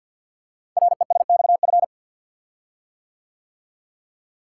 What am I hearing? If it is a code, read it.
Morse code, heard as a keyed tone, about 48 wpm, 690 Hz. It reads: RESXF
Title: Code Groups - 5 Letters - 50wpm